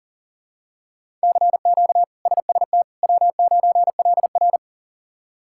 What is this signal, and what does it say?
Morse code, heard as a keyed tone, about 40 wpm, 700 Hz. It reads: CQ SST W9LR
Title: K1USN SST QSO Practice - 40wpm